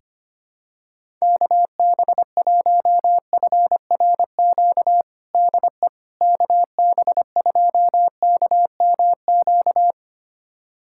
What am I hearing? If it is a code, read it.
Morse code, heard as a keyed tone, about 25 wpm, 700 Hz. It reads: KB1FRQ DE KB2KMQ
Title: USA Callsign DE Callsign - 6 Characters - 25wpm